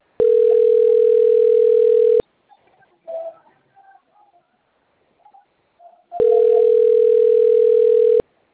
Ringback tone (North America), 440 + 480 Hz.
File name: out- - - - - .66955